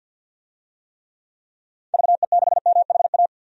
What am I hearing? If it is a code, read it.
Morse code, heard as a keyed tone, about 51 wpm, 700 Hz. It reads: VE6MHA
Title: DXCC-100 Call Signs - 6 Characters - 50wpm